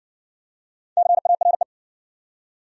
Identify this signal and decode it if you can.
Morse code, heard as a keyed tone, about 60 wpm, 700 Hz. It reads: CARE